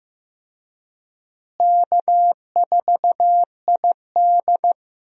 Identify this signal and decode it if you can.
Morse code, heard as a keyed tone, about 15 wpm, 700 Hz. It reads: K4ID
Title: USA Call Signs - 4 Characters - Repeated 2x - 15wpm